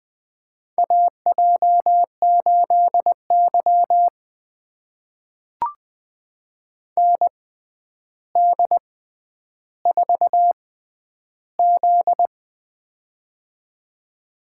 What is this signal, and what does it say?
Morse code, heard as a keyed tone, about 20 wpm, spaced as for 10 wpm, 700 Hz. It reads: AJ8Y ND4Z
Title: USA Call Signs - 4 Characters - 6x Character Spacing - 20wpm